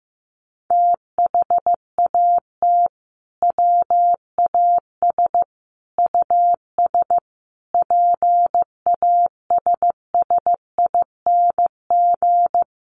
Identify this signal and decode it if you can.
Morse code, heard as a keyed tone, about 15 wpm, 700 Hz. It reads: THAT WAS US PASSING
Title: A Sound Of Thunder 15wpm